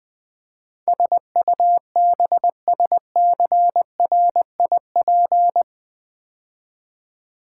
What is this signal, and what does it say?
Morse code, heard as a keyed tone, about 20 wpm, 700 Hz. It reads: SUBSCRIP
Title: Build A Word - 4 Letters At A Time - 12-Letter Words - 20wpm